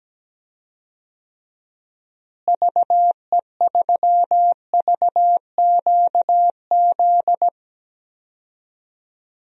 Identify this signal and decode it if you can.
Morse code, heard as a keyed tone, about 17 wpm, 700 Hz. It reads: VE3VQZ
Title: DXCC-100 Call Signs - 6 Characters - Repeated 2x - 17wpm